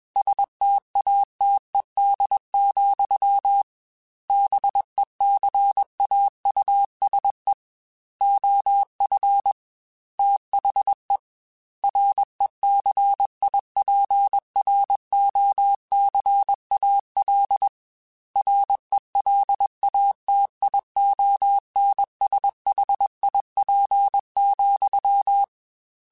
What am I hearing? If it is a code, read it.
Morse code, heard as a keyed tone, about 21 wpm, 800 Hz. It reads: STATED, BECAUSE OF THE RECIPROCAL RELATIONSHIP,
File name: Technical Subjects / Technical Subjects-mp3s / XC @20